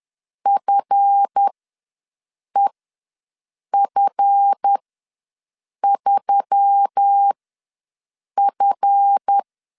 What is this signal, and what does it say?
Morse code, heard as a keyed tone, about 10 wpm, spaced as for 6 wpm, 800 Hz. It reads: FEF3F
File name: LEZIONI / 7 GF9Z / 7 GF ws 10wpm 3918